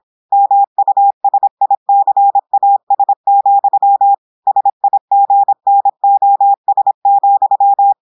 Morse code, heard as a keyed tone, about 26 wpm, 800 Hz. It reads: MUSICAS, SIGNOS,
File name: portugais texte en clair 2 vitesse 26 mots minutes